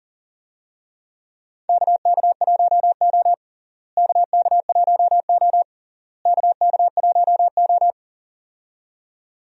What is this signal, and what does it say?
Morse code, heard as a keyed tone, about 40 wpm, 700 Hz. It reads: KK1O KK1O KK1O
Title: USA Call Signs - 4 Characters - Repeated 3x - 40wpm